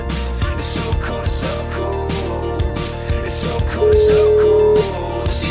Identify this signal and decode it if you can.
Ringback tone (Europe, 425 Hz), with noise.